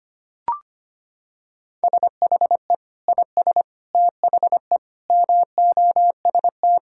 Morse code, heard as a keyed tone, about 25 wpm, 700 Hz. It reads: SHE IS THE MOST
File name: practice / 25wpm / Sentences with 2-Word Phrases - 25wpm